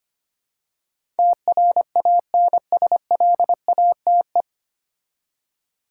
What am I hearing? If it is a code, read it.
Morse code, heard as a keyed tone, about 25 wpm, 690 Hz. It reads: TRANSLATE
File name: practice / 25wpm / 9 Letter Words - First 3 Letters Given - 25wpm